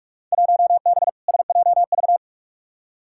Morse code, heard as a keyed tone, about 43 wpm, 700 Hz. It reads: 1B SJV
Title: Field Day Exchanges - 45wpm